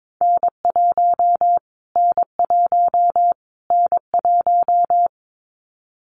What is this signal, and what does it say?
Morse code, heard as a keyed tone, about 22 wpm, 700 Hz. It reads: N1 N1 N1
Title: Callsign Prefixes for Top 25 DXCC - Repeated 3x - 22wpm